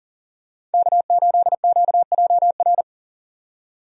Morse code, heard as a keyed tone, about 40 wpm, 700 Hz. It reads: K8QJR
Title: USA Call Signs - 5 Characters - 40wpm